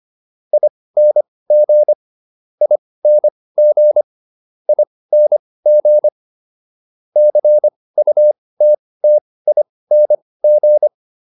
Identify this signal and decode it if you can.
Morse code, heard as a keyed tone, about 25 wpm, spaced as for 18 wpm, 600 Hz. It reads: ING ING ING CUTTING